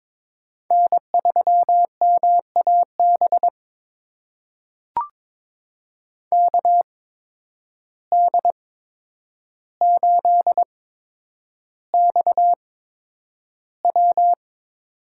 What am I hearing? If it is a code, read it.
Morse code, heard as a keyed tone, about 22 wpm, spaced as for 9 wpm, 700 Hz. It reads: N3MAB KD8XW
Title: USA Call Signs - 5 Characters - 8x Character Spacing - 22wpm